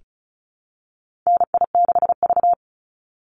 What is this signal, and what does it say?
Morse code, heard as a keyed tone, about 34 wpm, 700 Hz. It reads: NI6V
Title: Call Signs - 35wpm